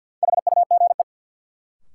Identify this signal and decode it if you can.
Morse code, heard as a keyed tone, about 53 wpm, 700 Hz. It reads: HUGE
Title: Sentences from Top 1100 Words - New Word Drill - 50wpm